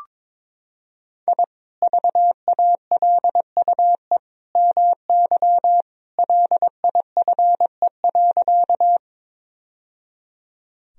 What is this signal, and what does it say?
Morse code, heard as a keyed tone, about 22 wpm, 700 Hz. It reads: I VALUE MY LIFE.
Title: Sentences from Top 1000 Words - Review Part 3 - 22wpm